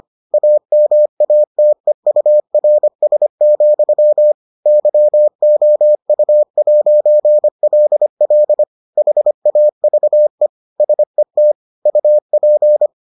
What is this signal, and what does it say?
Morse code, heard as a keyed tone, about 25 wpm, 600 Hz. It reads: AMATEURS, YOU'LL HAVE SET UP